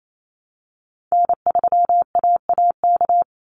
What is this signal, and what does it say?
Morse code, heard as a keyed tone, about 28 wpm, 700 Hz. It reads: N3AAK
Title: Call Signs Repeated 3x - 28wpm